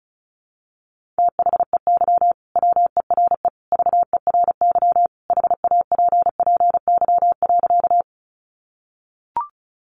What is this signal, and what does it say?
Morse code, heard as a keyed tone, about 34 wpm, 700 Hz. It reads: THEY WERE VERY HAPPY.
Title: Sentences from Top 1000 Words 35wpm